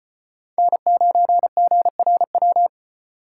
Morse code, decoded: N9GRW

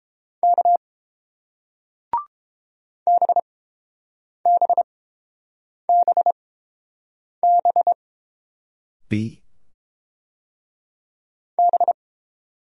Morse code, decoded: K BBBBB

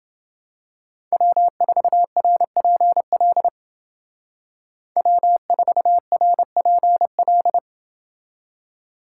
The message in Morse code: W4RPL W4RPL

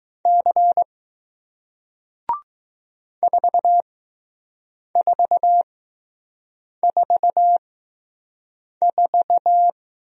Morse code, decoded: C 4444

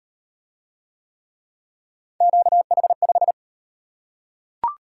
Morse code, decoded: QH5

